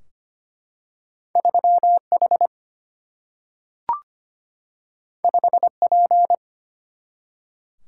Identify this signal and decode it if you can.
Morse code, heard as a keyed tone, about 25 wpm, 700 Hz. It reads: 3H 5P
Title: Single-Letter Single-Number 25wpm